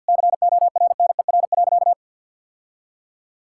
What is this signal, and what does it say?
Morse code, heard as a keyed tone, about 50 wpm, 700 Hz. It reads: CORNER.
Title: Sentences from Top 1600 Words 50wpm